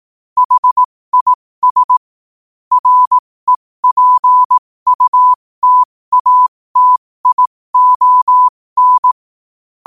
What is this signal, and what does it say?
Morse code, heard as a keyed tone, about 18 wpm, spaced as for 15 wpm, 1000 Hz. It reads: HIS REPUTATION
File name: qotdfiles / 15wpm